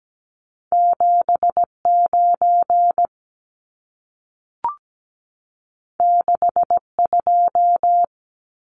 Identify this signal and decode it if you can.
Morse code, heard as a keyed tone, about 17 wpm, 700 Hz. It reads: 79 62
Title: Numbers - 2-Digit 17wpm